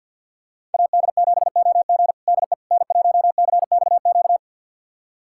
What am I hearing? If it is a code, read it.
Morse code, heard as a keyed tone, about 48 wpm, 690 Hz. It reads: AD7OG DE N1CXQ
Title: USA Callsign DE Callsign - 5 Characters - 50wpm